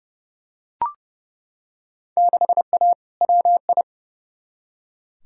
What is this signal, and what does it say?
Morse code, heard as a keyed tone, about 30 wpm, 700 Hz. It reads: 6A WI
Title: Field Day Exchanges - 30wpm